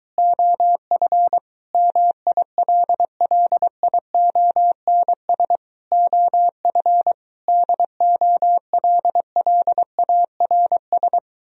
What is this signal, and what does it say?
Morse code, heard as a keyed tone, about 24 wpm, 700 Hz. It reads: OF MILLIONS OF DOLLARS